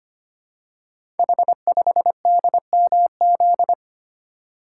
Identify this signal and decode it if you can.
Morse code, heard as a keyed tone, about 25 wpm, 700 Hz. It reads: H5DMZ
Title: Code Groups - 5 Letter:Numbers - 25wpm